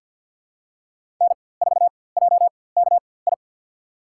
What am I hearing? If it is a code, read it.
Morse code, heard as a keyed tone, about 53 wpm, spaced as for 23 wpm, 700 Hz. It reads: N4JKI
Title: USA Call Signs - 5 Characters - 4x Character Spacing - 50wpm